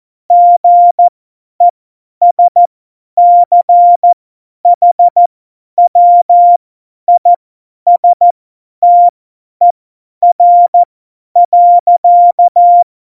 Morse code, heard as a keyed tone, about 14 wpm, spaced as for 10 wpm, 700 Hz. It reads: GESCHWISTER.
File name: Morsekurs / Witze / Witz 03. November 2020, 50 F70